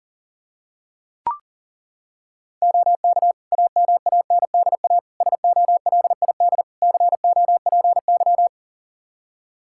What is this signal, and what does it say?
Morse code, heard as a keyed tone, about 40 wpm, 700 Hz. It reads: OK AMANDA SOLID COPY
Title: K1USN SST QSO Practice 40wpm